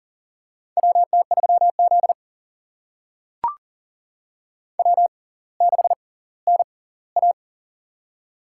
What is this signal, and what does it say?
Morse code, heard as a keyed tone, about 38 wpm, spaced as for 20 wpm, 700 Hz. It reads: WT3Z W6NA
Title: USA Call Signs - 4 Characters - 6x Character Spacing - 40wpm